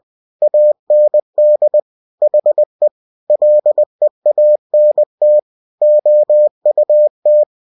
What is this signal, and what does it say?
Morse code, heard as a keyed tone, about 20 wpm, 600 Hz. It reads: AND HE LEANT OUT
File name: cw-books / winnie-the-pooh / winnie-20wpm-page-0092